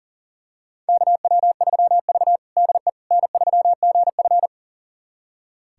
Morse code, heard as a keyed tone, about 40 wpm, 700 Hz. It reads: KW3V DE N3GF
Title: USA Callsign DE Callsign - 4 Characters - 40wpm